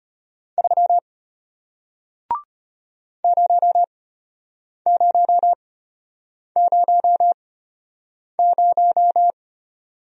Morse code, heard as a keyed tone, about 33 wpm, spaced as for 8 wpm, 700 Hz. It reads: 3 0000